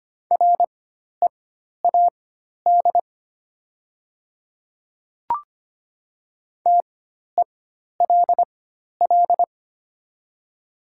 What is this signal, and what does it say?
Morse code, heard as a keyed tone, about 25 wpm, spaced as for 12 wpm, 700 Hz. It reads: READ TELL
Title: 4x Character Spacing - Top 500 Words - 25wpm